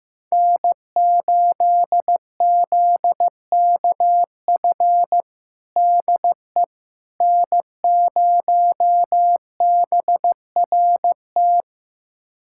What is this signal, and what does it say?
Morse code, heard as a keyed tone, about 16 wpm, 700 Hz. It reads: N8ZKF DE N0BRT
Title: USA Callsign DE Callsign - 5 Characters - 15wpm